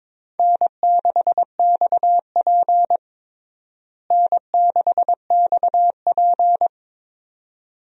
Morse code, heard as a keyed tone, about 22 wpm, 700 Hz. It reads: N6XP N6XP
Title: USA Call Signs - 4 Characters - Repeated 2x - 22wpm